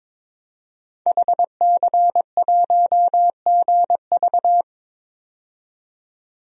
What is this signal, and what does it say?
Morse code, heard as a keyed tone, about 22 wpm, 700 Hz. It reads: HC1GV
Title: DXCC-100 Call Signs - 5 Characters - 22wpm